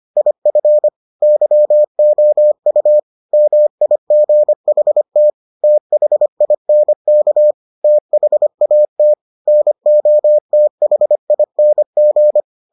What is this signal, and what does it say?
Morse code, heard as a keyed tone, about 25 wpm, 600 Hz. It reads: IF YOU MIGHT THINK THAT NOTHING